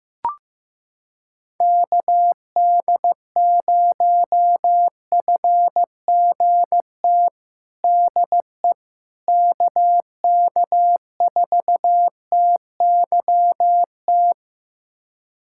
Morse code, heard as a keyed tone, about 16 wpm, 700 Hz. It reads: KD0FGT DE KK4TYT
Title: USA Callsign DE Callsign - 6 Characters - 15wpm